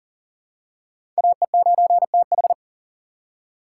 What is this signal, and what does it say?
Morse code, heard as a keyed tone, about 40 wpm, 700 Hz. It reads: AE9TH
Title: USA Call Signs - 6x Character Spacing - 40wpm